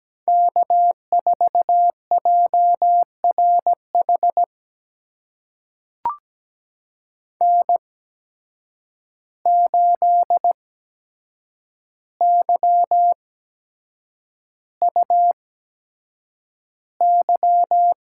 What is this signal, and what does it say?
Morse code, heard as a keyed tone, about 17 wpm, spaced as for 7 wpm, 700 Hz. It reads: K4JRH N8YUY
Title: USA Call Signs - 5 Characters - 8x Character Spacing - 17wpm